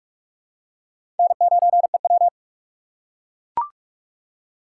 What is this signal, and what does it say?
Morse code, heard as a keyed tone, about 44 wpm, 700 Hz. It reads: N9EW
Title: USA Call Signs - 4 Characters - 2x Character Spacing - 45wpm